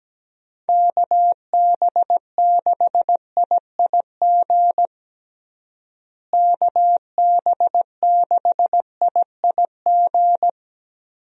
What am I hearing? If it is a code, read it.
Morse code, heard as a keyed tone, about 17 wpm, 700 Hz. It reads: KB6IIG KB6IIG
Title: USA Call Signs - 6 Characters - Repeated 2x - 17wpm